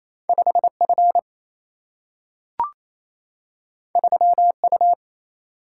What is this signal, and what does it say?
Morse code, heard as a keyed tone, about 28 wpm, 700 Hz. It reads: 5 F 3 U
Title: Single-Letter Single-Number 28wpm